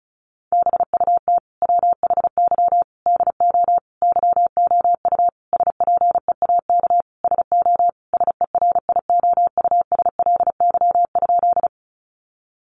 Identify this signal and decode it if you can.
Morse code, heard as a keyed tone, about 34 wpm, 700 Hz. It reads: BUT WHY DO YOU SPEAK SO SERIOUSLY?
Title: Sentences from Top 1200 Words 35wpm